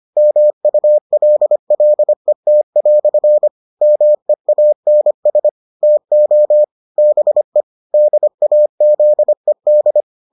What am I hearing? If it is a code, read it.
Morse code, heard as a keyed tone, about 25 wpm, 600 Hz. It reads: MULLET" MEANS TO BE DAZED